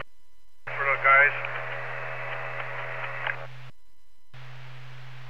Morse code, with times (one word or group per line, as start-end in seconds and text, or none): none